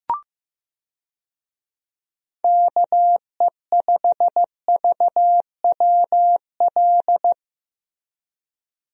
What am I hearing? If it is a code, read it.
Morse code, heard as a keyed tone, about 16 wpm, 700 Hz. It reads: KE5VWL
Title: USA Call Signs - 6 Characters - Repeated 2x - 15wpm